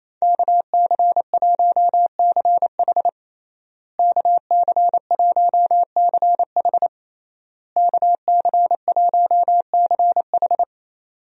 Morse code, heard as a keyed tone, about 28 wpm, 700 Hz. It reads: KC1CH KC1CH KC1CH
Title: USA Call Signs - 5 Characters - Repeated 3x - 28wpm